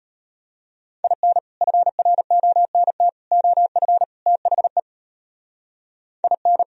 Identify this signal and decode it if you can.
Morse code, heard as a keyed tone, about 32 wpm, 700 Hz. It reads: IN FRONT OF THE IN